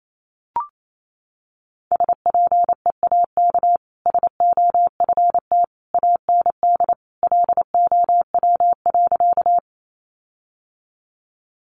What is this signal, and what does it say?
Morse code, heard as a keyed tone, about 28 wpm, 700 Hz. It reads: SPEAK SOFT AND LOW.
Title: Sentences from Top 1500 Words 28wpm